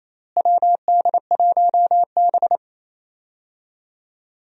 Morse code, heard as a keyed tone, about 27 wpm, 700 Hz. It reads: WD1B
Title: USA Call Signs - 4 Characters - Repeated 2x - 28wpm